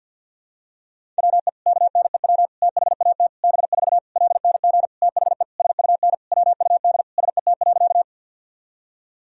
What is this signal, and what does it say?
Morse code, heard as a keyed tone, about 49 wpm, 700 Hz. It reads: WE KNEW THAT B4 LNG THE SUN WUD SET.
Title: Sentences from Top 400 Words and Abbreviations 50wpm